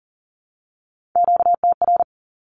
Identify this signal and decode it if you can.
Morse code, heard as a keyed tone, about 40 wpm, 700 Hz. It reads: QTR
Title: Q-Codes Repeated 3x 40wpm